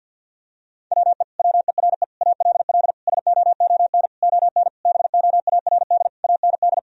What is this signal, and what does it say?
Morse code, heard as a keyed tone, about 50 wpm, 700 Hz. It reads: WE WERE ALL SOON ON BOARD AND
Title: Sentences from Top 1000 Words - Review Part 4 - 50wpm